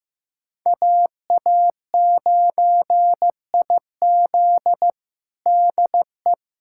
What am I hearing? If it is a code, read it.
Morse code, heard as a keyed tone, about 16 wpm, 700 Hz. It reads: AA9IZ DE
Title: USA Callsign DE Callsign - 5 Characters - 15wpm